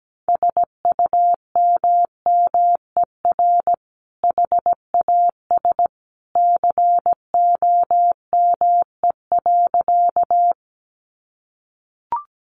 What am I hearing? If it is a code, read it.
Morse code, heard as a keyed tone, about 17 wpm, 700 Hz. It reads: SUMMER HAS COME.
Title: Sentences from Top 1100 Words 17wpm